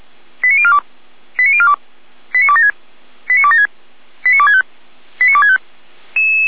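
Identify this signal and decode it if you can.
Five-tone selective call (ZVEI), heard as quick runs of five tones, tones of about 70 ms. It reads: 89042 89042 89276 89276 89265 89265
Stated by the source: ZVEI five-tone call received by SDR, from feuerwehr-orlamuende.org archive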